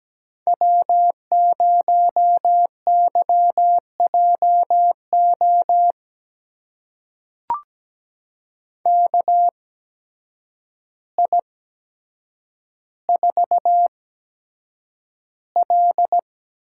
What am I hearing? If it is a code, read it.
Morse code, heard as a keyed tone, about 17 wpm, spaced as for 8 wpm, 700 Hz. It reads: W0YJO KI4L
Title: USA Call Signs - 8x Character Spacing - 17wpm